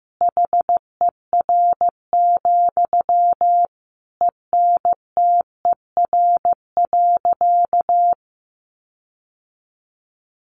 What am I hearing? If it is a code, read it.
Morse code, heard as a keyed tone, about 15 wpm, 700 Hz. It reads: HER, ENTER.
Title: Sentences from Top 1300 Words 15wpm